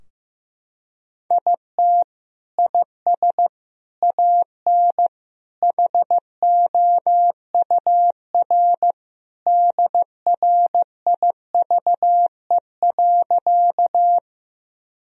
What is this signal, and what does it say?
Morse code, heard as a keyed tone, about 16 wpm, 700 Hz. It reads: IT IS AN HOUR DRIVE.
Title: Sentences from Top 1000 Words - Review Part 2 - 15wpm